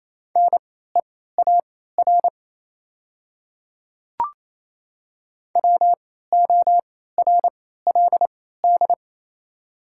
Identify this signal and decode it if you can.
Morse code, heard as a keyed tone, about 27 wpm, spaced as for 16 wpm, 700 Hz. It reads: NEAR WORLD
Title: 3x Character Spacing - Top 500 Words - 28wpm